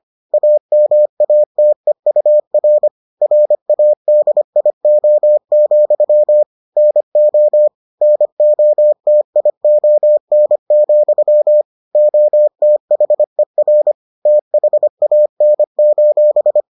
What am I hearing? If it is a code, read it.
Morse code, heard as a keyed tone, about 25 wpm, 600 Hz. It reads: AMATEUR RADIO, NO NOTION, OTHER THAN: